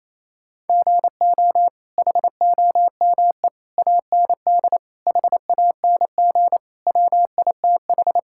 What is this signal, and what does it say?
Morse code, heard as a keyed tone, about 28 wpm, 700 Hz. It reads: GO HOME AND HANG WITH
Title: Sentences from Top 1600 Words 28wpm